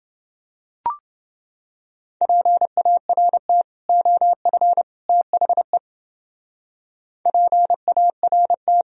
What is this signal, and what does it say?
Morse code, heard as a keyed tone, about 29 wpm, 700 Hz. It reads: PART OF THE PART